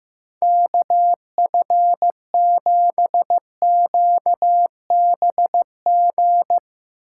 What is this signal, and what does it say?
Morse code, heard as a keyed tone, about 15 wpm, 700 Hz. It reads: KF7QBG
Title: USA Call Signs - 6 Characters - 8x Character Spacing - 15wpm